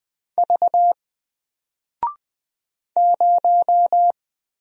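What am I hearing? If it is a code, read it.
Morse code, heard as a keyed tone, about 20 wpm, 700 Hz. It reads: V 0
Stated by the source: Single letter and number Morse drill at 20 wpm, from N4LCD.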